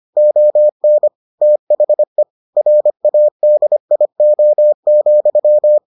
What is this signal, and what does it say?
Morse code, heard as a keyed tone, about 25 wpm, 600 Hz. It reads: ON THE RADIO,